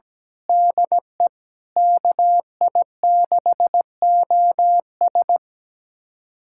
Morse code, heard as a keyed tone, about 17 wpm, 700 Hz. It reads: DE KI6OS
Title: USA Callsign DE Callsign - 5 Characters - 17wpm